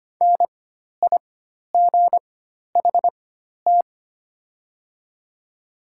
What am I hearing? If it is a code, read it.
Morse code, heard as a keyed tone, about 25 wpm, spaced as for 12 wpm, 700 Hz. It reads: NIGHT
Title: 4x Character Spacing - Top 300 Words - 25wpm